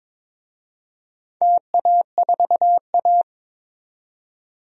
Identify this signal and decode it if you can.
Morse code, heard as a keyed tone, about 22 wpm, 700 Hz. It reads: TA4A
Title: DXCC-100 Call Signs - 4 Characters - 22wpm